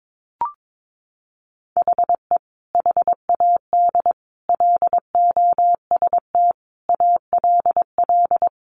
Morse code, heard as a keyed tone, about 22 wpm, 700 Hz. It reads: HE HAD LOST ALL
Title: Sentences from Top 1400 Words 22wpm